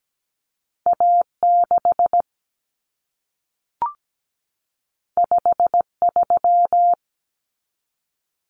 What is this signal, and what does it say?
Morse code, heard as a keyed tone, about 17 wpm, 700 Hz. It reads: A6 53
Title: Common Cut Numbers - 2-Digit 17wpm